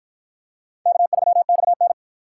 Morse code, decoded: K3XN

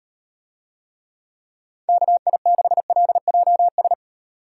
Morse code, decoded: KI6LJS

